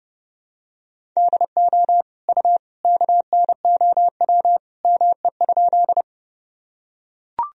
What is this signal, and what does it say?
Morse code, heard as a keyed tone, about 30 wpm, 700 Hz. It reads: DO U KNOW ME?
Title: Sentences from Top 200 Words and Abbreviations 30wpm